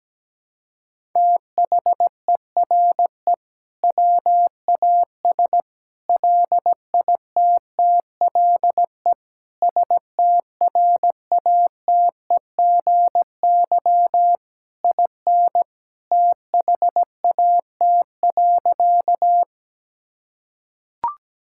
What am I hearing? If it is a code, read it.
Morse code, heard as a keyed tone, about 17 wpm, 700 Hz. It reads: THERE WAS LITTLE STRATEGY IN THAT.
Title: Sentences from Top 1000 Words - Review Part 5 - 17wpm